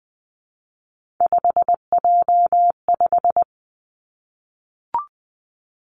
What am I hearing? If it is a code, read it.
Morse code, heard as a keyed tone, about 20 wpm, 700 Hz. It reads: 5J5
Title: Three Challenging Letter-Number Combinations 20wpm